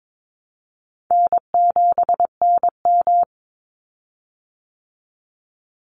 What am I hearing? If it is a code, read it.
Morse code, heard as a keyed tone, about 22 wpm, 700 Hz. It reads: N7NM
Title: Call Signs Repeated 3x - 22wpm